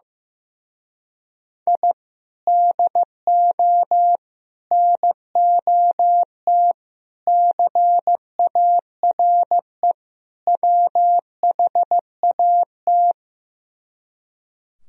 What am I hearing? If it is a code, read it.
Morse code, heard as a keyed tone, about 16 wpm, 700 Hz. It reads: I DO NOT CARE WHAT
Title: Speed-Racing - Sets of 5 Words 15wpm